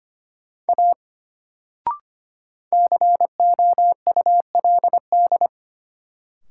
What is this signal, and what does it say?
Morse code, heard as a keyed tone, about 25 wpm, 700 Hz. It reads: A COULD
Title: Sentences from Top 100 Words - New Word Drill - 25wpm